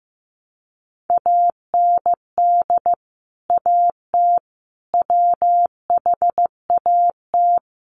Morse code, heard as a keyed tone, about 15 wpm, 700 Hz. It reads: AND AT WHAT
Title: Sentences from Top 1200 Words 15wpm